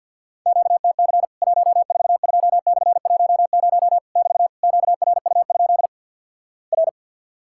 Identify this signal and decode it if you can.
Morse code, heard as a keyed tone, about 50 wpm, 700 Hz. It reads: QTC 142/10 <BT> QRU? R
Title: Worked All Europe DX Contest - QTC Exchange with Repeats - 50wpm